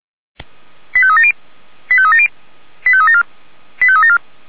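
Five-tone selective call (ZVEI), heard as quick runs of five tones, tones of about 70 ms. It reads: 85370 85370 85374 85374
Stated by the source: ZVEI five-tone call received by SDR.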